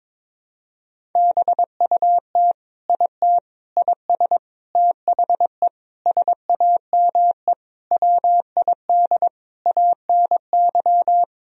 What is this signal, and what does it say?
Morse code, heard as a keyed tone, about 22 wpm, 700 Hz. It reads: BUT IT IS THE SAME WID ANY
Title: Sentences from Top 300 Words and Abbreviations 22wpm